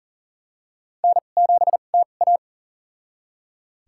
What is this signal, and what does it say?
Morse code, heard as a keyed tone, about 40 wpm, spaced as for 29 wpm, 700 Hz. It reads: N7TA
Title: USA Call Signs - 4 Characters - 2x Character Spacing - 40wpm